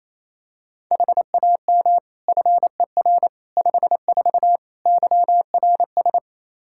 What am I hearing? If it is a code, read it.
Morse code, heard as a keyed tone, about 28 wpm, 700 Hz. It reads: HAM FER 54 YRS
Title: K1USN SST QSO Practice 28wpm